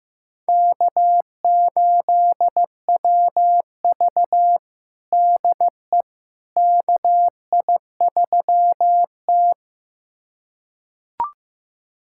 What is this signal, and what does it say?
Morse code, heard as a keyed tone, about 15 wpm, 700 Hz. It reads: K8WV DE KI3T
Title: USA Callsign DE Callsign - 4 Characters - 15wpm